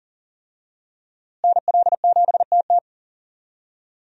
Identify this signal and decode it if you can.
Morse code, heard as a keyed tone, about 40 wpm, 700 Hz. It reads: NL7TT
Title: USA Call Signs - 5 Characters - Repeated 2x - 40wpm